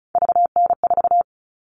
Morse code, heard as a keyed tone, about 34 wpm, 700 Hz. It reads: VN4
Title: All Cut Numbers - 3-Digit 35wpm